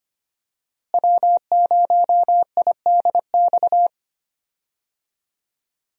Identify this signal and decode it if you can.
Morse code, heard as a keyed tone, about 25 wpm, 700 Hz. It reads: W0IDX
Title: USA Call Signs - 5 Characters - Repeated 3x - 25wpm